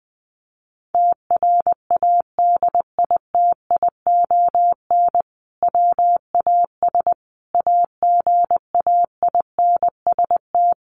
Morse code, heard as a keyed tone, about 20 wpm, 700 Hz. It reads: TRADITION WAS AGAINST
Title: Sentences from Top 1500 Words 20wpm